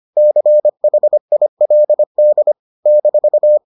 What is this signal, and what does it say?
Morse code, heard as a keyed tone, about 25 wpm, 600 Hz. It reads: CHILD -